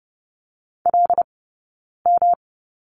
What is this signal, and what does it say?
Morse code, heard as a keyed tone, about 30 wpm, 700 Hz. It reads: L M
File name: Lesson 19d - Letter w - Two Characters 30wpm